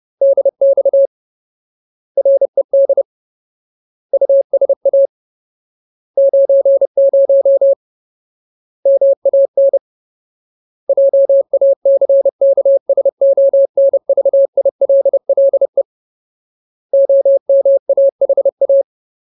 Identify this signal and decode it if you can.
Morse code, heard as a keyed tone, about 30 wpm, 550 Hz. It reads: DX RED USA 90 MAN JACKSONVILLE OMAHA
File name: Ultimate Word List / 30WPM Ultimate